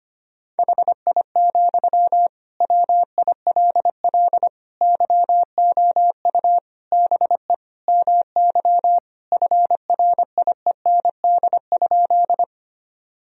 Morse code, heard as a keyed tone, about 25 wpm, 700 Hz. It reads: HI, WILL YOU BE MY FRIEND?
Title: Sentences from Top 1000 Words - Review Part 3 - 25wpm